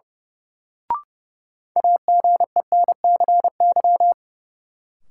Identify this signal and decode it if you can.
Morse code, heard as a keyed tone, about 31 wpm, 700 Hz. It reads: AGENCY
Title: Sentences from Top 1300 Words - New Word Drill Alt - 30wpm